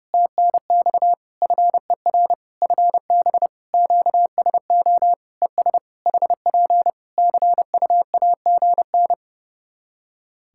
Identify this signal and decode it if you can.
Morse code, heard as a keyed tone, about 30 wpm, 700 Hz. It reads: TNX FER FB QSO ES HP CUAGN
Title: Practice QSO - Exchange 4 - Ending - 30wpm